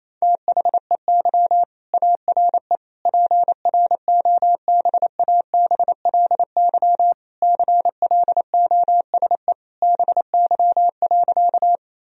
Morse code, heard as a keyed tone, about 28 wpm, 700 Hz. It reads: THEY ARE PROBABLY CLOSE BY.
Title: Sentences from Top 1000 Words - Review Part 4 - 28wpm